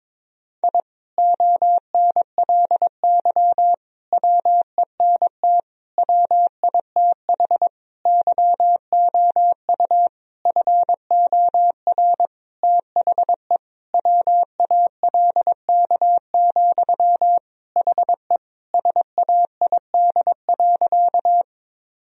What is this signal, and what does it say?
Morse code, heard as a keyed tone, about 22 wpm, 700 Hz. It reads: I ONLY WENT WITH YOU FOR THE WALK, HE SAID.
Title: Sentences from Top 1000 Words - Review Part 1 - 22wpm